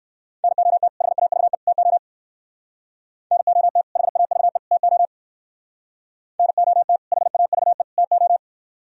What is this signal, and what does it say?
Morse code, heard as a keyed tone, about 62 wpm, 700 Hz. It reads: NOT HAVE TO NOT HAVE TO NOT HAVE TO